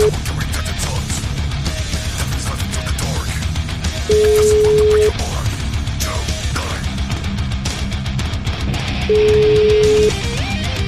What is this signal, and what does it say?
Ringback tone (Europe, 425 Hz), with noise.